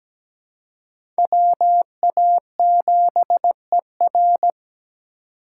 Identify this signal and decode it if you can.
Morse code, heard as a keyed tone, about 20 wpm, 700 Hz. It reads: WA7ER
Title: Speed Racing - USA Call Signs 17wpm